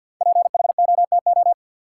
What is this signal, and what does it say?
Morse code, heard as a keyed tone, about 53 wpm, 700 Hz. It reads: PHOTO
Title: Sentences from Top 1200 Words - New Word Drill - 50wpm